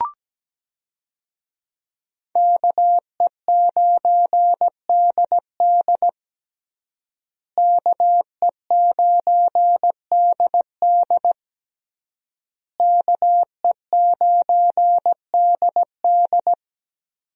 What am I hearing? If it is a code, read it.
Morse code, heard as a keyed tone, about 17 wpm, 700 Hz. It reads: KE9DD KE9DD KE9DD
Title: USA Call Signs - 5 Characters - Repeated 3x - 17wpm